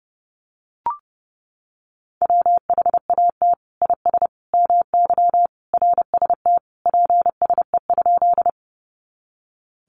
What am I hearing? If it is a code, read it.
Morse code, heard as a keyed tone, about 30 wpm, 700 Hz. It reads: WHAT IS MY RST PSE?
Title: Common QSO Phrases 30wpm